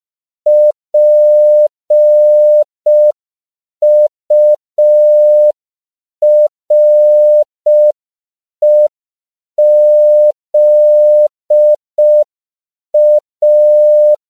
Morse code, heard as a keyed tone, about 5 wpm, 600 Hz. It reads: PUREZA
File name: CW / 5PPM